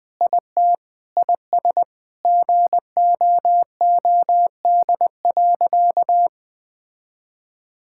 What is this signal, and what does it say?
Morse code, heard as a keyed tone, about 20 wpm, 700 Hz. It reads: IT IS GOOD.
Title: Sentences with 3-Word Phrases - Spoken Beginning - 20wpm